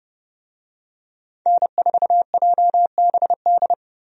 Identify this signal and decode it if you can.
Morse code, heard as a keyed tone, about 30 wpm, 700 Hz. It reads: N4JBD
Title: USA Call Signs - 5 Characters - 30wpm